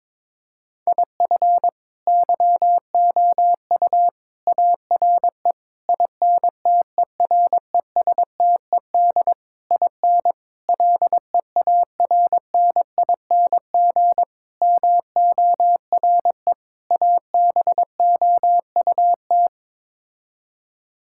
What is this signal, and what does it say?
Morse code, heard as a keyed tone, about 22 wpm, 700 Hz. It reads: IF YOU ARE INTERESTED IN LEARNING MORE ABOUT
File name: practice / 22wpm / Sets of 8 Words - 22wpm